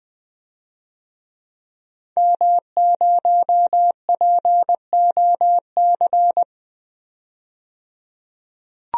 Morse code, heard as a keyed tone, about 21 wpm, 700 Hz. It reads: M0POC
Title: DXCC-100 Call Signs - 5 Characters - 20wpm